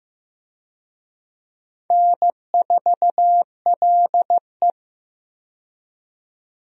Morse code, heard as a keyed tone, about 15 wpm, 700 Hz. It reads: N4LE
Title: USA Call Signs - 4 Characters - 15wpm